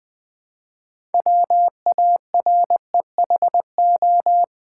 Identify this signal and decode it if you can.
Morse code, heard as a keyed tone, about 20 wpm, 700 Hz. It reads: WAREHO
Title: Build A Word - 3 Letters At A Time - 9-Letter Words - 20wpm